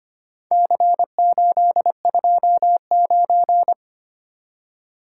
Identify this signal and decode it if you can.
Morse code, heard as a keyed tone, about 25 wpm, 700 Hz. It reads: C829
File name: practice / 25wpm / Four Letter-Number Combinations - 25wpm